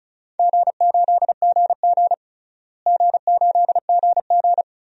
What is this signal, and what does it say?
Morse code, heard as a keyed tone, about 36 wpm, 700 Hz. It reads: G8GG G8GG
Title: DXCC-100 Call Signs - 4 Characters - Repeated 2x - 35wpm